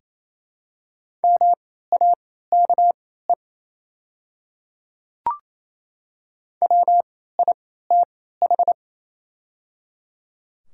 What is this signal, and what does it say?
Morse code, heard as a keyed tone, about 28 wpm, spaced as for 16 wpm, 700 Hz. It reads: MAKE WITH
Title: 3x Character Spacing - Top 100 Words - 28wpm